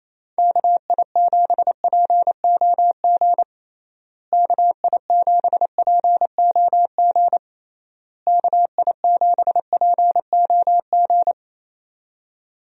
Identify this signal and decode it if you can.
Morse code, heard as a keyed tone, about 28 wpm, 700 Hz. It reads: KI7POG KI7POG KI7POG
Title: USA Call Signs - 6 Characters - Repeated 3x - 28wpm